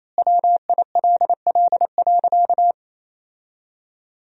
Morse code, heard as a keyed tone, about 28 wpm, 700 Hz. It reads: WILL.
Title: 3x Word-Spacing - Sentences from Top 200 Words - 28wpm